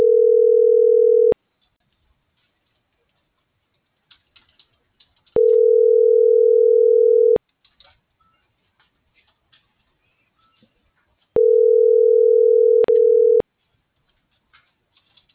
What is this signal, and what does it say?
Ringback tone (North America), 440 + 480 Hz.